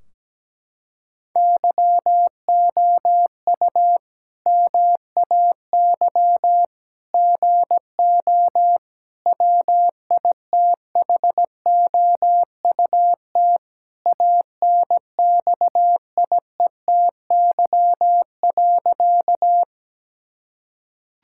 Morse code, heard as a keyed tone, about 17 wpm, 700 Hz. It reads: YOU MAY GO WITHOUT ANXIETY.
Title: Sentences from Top 1600 Words 17wpm